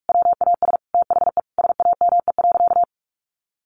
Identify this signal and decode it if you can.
Morse code, heard as a keyed tone, about 44 wpm, 700 Hz. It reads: WAS THE SAME.
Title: Sentences from Top 800 Words 45wpm